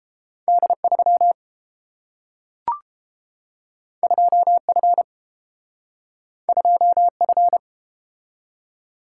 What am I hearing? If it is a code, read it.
Morse code, heard as a keyed tone, about 30 wpm, 700 Hz. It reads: D3 2F 2F